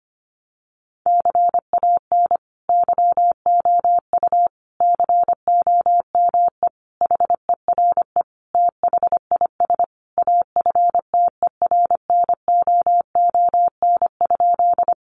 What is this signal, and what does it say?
Morse code, heard as a keyed tone, about 25 wpm, 700 Hz. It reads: CAN YOU COME HERE THIS AFTERNOON?
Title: Sentences from Top 1500 Words 25wpm